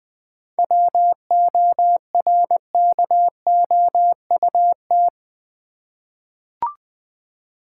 Morse code, heard as a keyed tone, about 20 wpm, 700 Hz. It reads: WORKOUT
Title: 7 Letter Words - First 3 Letters Given - 20wpm